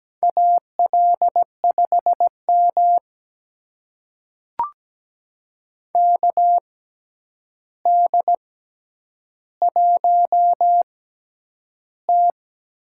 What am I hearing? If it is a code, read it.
Morse code, heard as a keyed tone, about 17 wpm, spaced as for 9 wpm, 700 Hz. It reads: AL5M KD1T
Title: USA Call Signs - 4 Characters - 6x Character Spacing - 17wpm